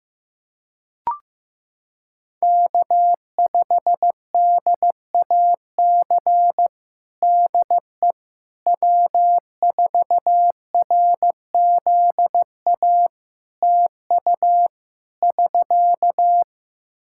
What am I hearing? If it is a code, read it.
Morse code, heard as a keyed tone, about 15 wpm, 700 Hz. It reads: K5DAC DE W4RZA TU <SK>
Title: Practice QSO - Exchange 4 - Ending - 15wpm